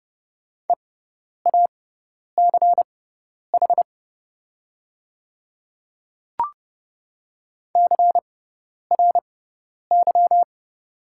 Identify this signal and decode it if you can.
Morse code, heard as a keyed tone, about 30 wpm, spaced as for 10 wpm, 690 Hz. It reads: EACH CRY
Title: 6x Character Spacing - Top 500 Words - 30wpm